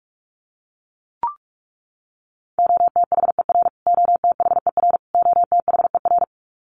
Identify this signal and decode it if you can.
Morse code, heard as a keyed tone, about 44 wpm, 700 Hz. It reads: OTHER OTHER OTHER
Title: Top 500 Words Repeated 45wpm